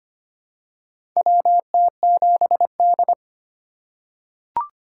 Morse code, heard as a keyed tone, about 25 wpm, 700 Hz. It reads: WT7D
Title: USA Call Signs - 4 Characters - 4x Character Spacing - 25wpm